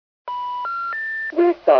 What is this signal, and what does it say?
Special information tone, with speech.